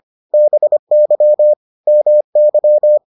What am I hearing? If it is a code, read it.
Morse code, heard as a keyed tone, about 25 wpm, 600 Hz. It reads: BY MY